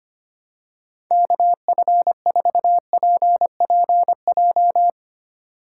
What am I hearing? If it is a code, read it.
Morse code, heard as a keyed tone, about 25 wpm, 700 Hz. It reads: KF4PPJ
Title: USA Call Signs - 6 Characters - 2x Character Spacing - 25wpm